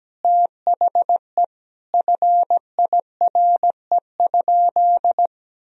Morse code, heard as a keyed tone, about 17 wpm, 700 Hz. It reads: THE FIRE?
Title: Sentences from Top 1000 Words - Review Part 3 - 17wpm